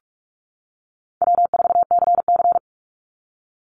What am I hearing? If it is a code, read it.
Morse code, heard as a keyed tone, about 44 wpm, 700 Hz. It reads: W4CC
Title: Easy Call Signs - 45wpm